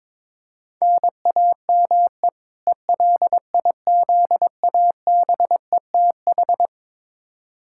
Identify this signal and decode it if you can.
Morse code, heard as a keyed tone, about 22 wpm, 700 Hz. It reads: NAME ELIZABETH
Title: K1USN SST QSO Practice 22wpm